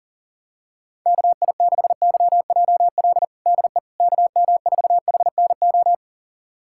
Morse code, heard as a keyed tone, about 40 wpm, 700 Hz. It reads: KI6YJL DE KM4HNO
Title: USA Callsign DE Callsign - 6 Characters - 40wpm